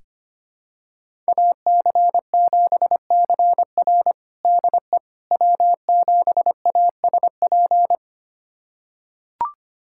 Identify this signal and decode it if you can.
Morse code, heard as a keyed tone, about 25 wpm, 700 Hz. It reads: AC7CR DE W7ASP
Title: USA Callsign DE Callsign - 5 Characters - 25wpm